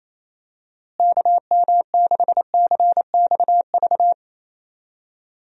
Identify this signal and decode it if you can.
Morse code, heard as a keyed tone, about 27 wpm, 700 Hz. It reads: KM6CXV
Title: USA Call Signs - 6 Characters - Repeated 3x - 28wpm